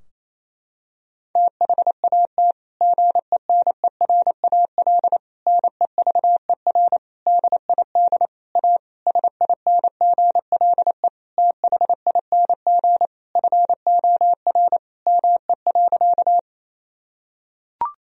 Morse code, heard as a keyed tone, about 28 wpm, 700 Hz. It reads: THAT GENERAL NEVER DID A SINGLE THING FOR ME.
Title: Sentences from Top 1000 Words - Review Part 4 - 28wpm